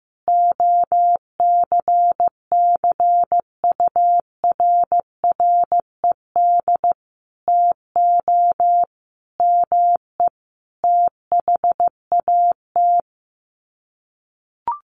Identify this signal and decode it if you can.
Morse code, heard as a keyed tone, about 15 wpm, 700 Hz. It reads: OCCURRED TO ME THAT
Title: Sets of 5 Words Encore 15wpm